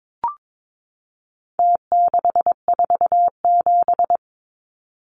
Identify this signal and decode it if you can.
Morse code, heard as a keyed tone, about 22 wpm, 700 Hz. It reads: T647
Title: Numbers - Contest Style - 4-Digit 22wpm